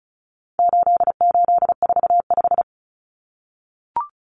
Morse code, heard as a keyed tone, about 34 wpm, 700 Hz. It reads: 8845
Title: Numbers - 4-Digit 35wpm